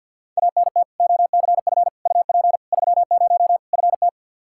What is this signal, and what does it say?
Morse code, heard as a keyed tone, about 47 wpm, 700 Hz. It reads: ANT OCF UP 30 FT
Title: Practice QSO - Exchange 2 - Info - 50wpm